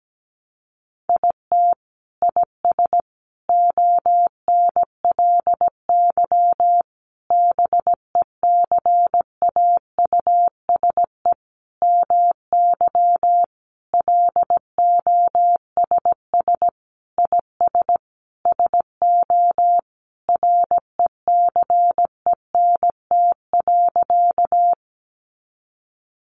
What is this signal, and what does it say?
Morse code, heard as a keyed tone, about 17 wpm, 700 Hz. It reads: IT IS ONLY BECAUSE MY LOSS IS SO RECENT.
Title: Sentences from Top 1400 Words 17wpm